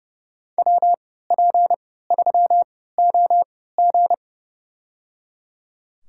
Morse code, heard as a keyed tone, about 30 wpm, spaced as for 17 wpm, 700 Hz. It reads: WP3OG
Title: USA Call Signs - 5 Characters - 3x Character Spacing - 30wpm